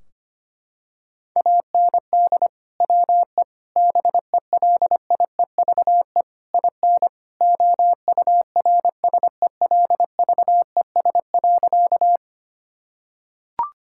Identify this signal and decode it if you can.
Morse code, heard as a keyed tone, about 25 wpm, 700 Hz. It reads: AND WE BELIEVE IN OURSELVES.
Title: Sentences from Top 1600 Words 25wpm